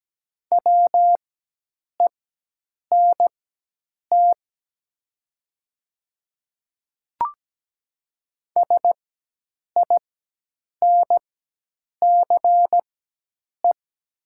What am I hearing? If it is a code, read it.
Morse code, heard as a keyed tone, about 17 wpm, spaced as for 8 wpm, 700 Hz. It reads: WENT SINCE
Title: 4x Character Spacing - Top 300 Words - 17wpm